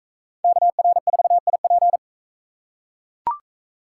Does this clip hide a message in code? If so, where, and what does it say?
Yes: Morse code from 0.4 to 2.0 s, heard as a keyed tone, about 42 wpm, 700 Hz. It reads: KR4IP